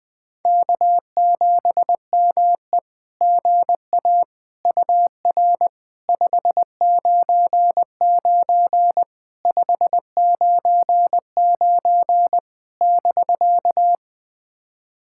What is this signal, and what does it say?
Morse code, heard as a keyed tone, about 20 wpm, 700 Hz. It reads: K7ME GA UR 599 599 <BK>
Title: SOTA QSO Practice 20wpm